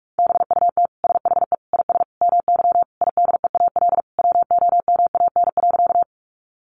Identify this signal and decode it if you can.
Morse code, heard as a keyed tone, about 44 wpm, 700 Hz. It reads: BUT SHE IS MY IDEAL WOMAN.
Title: Sentences from Top 1400 Words 45wpm